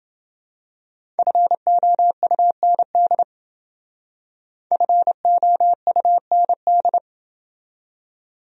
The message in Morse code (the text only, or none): FOUND FOUND